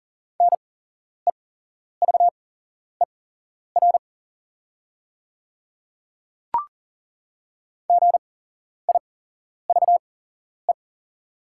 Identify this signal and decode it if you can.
Morse code, heard as a keyed tone, about 41 wpm, spaced as for 11 wpm, 700 Hz. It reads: NEVER GIVE
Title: 8x Character Spacing - Top 300 Words - 40wpm